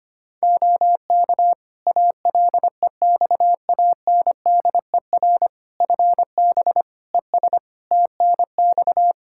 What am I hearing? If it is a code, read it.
Morse code, heard as a keyed tone, about 25 wpm, 700 Hz. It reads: OK ALEXANDER FB ES TNX
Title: Practice QSO - Exchange 2 - Info - 25wpm